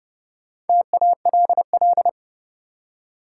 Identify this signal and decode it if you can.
Morse code, heard as a keyed tone, about 30 wpm, 700 Hz. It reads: TALL
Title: Four Letter Words - 30wpm